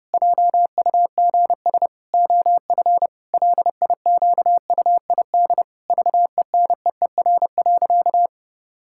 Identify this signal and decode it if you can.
Morse code, heard as a keyed tone, about 30 wpm, 700 Hz. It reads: JUGS OF LIQUID VENEER.
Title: Pangram Challenge v2 - 30wpm